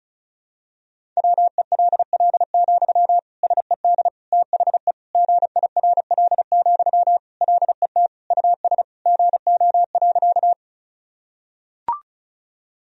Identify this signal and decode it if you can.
Morse code, heard as a keyed tone, about 36 wpm, 700 Hz. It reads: WELL, SED THE GIRL, LET US GO.
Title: Sentences from Top 400 Words and Abbreviations 35wpm